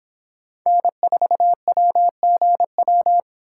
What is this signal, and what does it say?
Morse code, heard as a keyed tone, about 28 wpm, 700 Hz. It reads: N4WGW